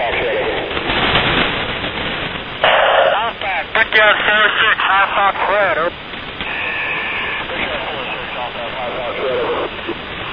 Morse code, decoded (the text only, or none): none